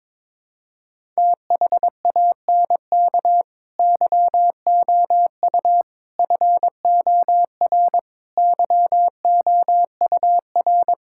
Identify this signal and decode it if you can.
Morse code, heard as a keyed tone, about 22 wpm, 700 Hz. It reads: THANK YOU FOR YOUR
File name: Difficult Sentences with Hints - 22wpm